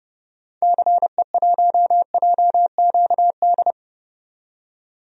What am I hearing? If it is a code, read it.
Morse code, heard as a keyed tone, about 30 wpm, 700 Hz. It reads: CE1JQD
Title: DXCC-100 Call Signs - 6 Characters - 30wpm